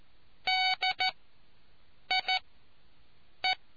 Morse code, heard as a keyed tone, about 14 wpm, spaced as for 6 wpm, 750 Hz. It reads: DIE